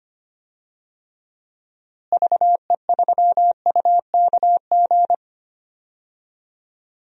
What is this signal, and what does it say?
Morse code, heard as a keyed tone, about 25 wpm, 700 Hz. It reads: VE3UKG
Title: DXCC-100 Call Signs - 6 Characters - Repeated 3x - 25wpm